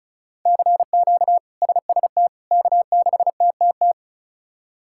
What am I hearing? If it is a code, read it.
Morse code, heard as a keyed tone, about 36 wpm, 700 Hz. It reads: CQ SST K6TTT
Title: K1USN SST QSO Practice - 35wpm